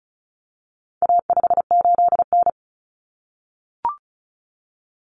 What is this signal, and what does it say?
Morse code, heard as a keyed tone, about 34 wpm, 700 Hz. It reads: A58N
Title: Common Cut Numbers - 4-Digit 35wpm